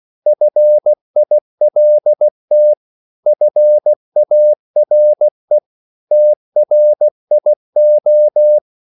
Morse code, heard as a keyed tone, about 16 wpm, 600 Hz. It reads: FILT FARE TRIO
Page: Korte danske ord 16wpm